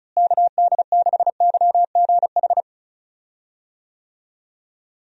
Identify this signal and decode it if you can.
Morse code, heard as a keyed tone, about 36 wpm, 700 Hz. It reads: KD6YGH
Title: USA Call Signs - 6 Characters - Repeated 2x - 35wpm